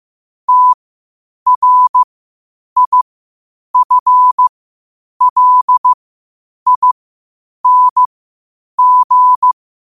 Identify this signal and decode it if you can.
Morse code, heard as a keyed tone, about 15 wpm, spaced as for 8 wpm, 1000 Hz. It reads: TRIFLING
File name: qotdfiles / 8wpm